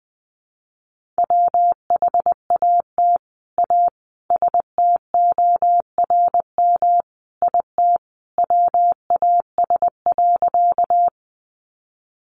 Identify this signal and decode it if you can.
Morse code, heard as a keyed tone, about 19 wpm, 700 Hz. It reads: WHAT A STORM IT WAS.
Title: Sentences from Top 1500 Words 20wpm